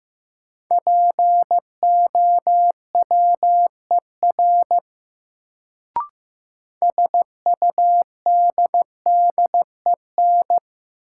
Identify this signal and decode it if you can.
Morse code, heard as a keyed tone, about 15 wpm, 700 Hz. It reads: POWER SUDDEN
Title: Sets of 4 Words - New Word Drill - 15wpm